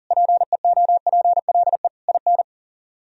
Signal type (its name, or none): Morse code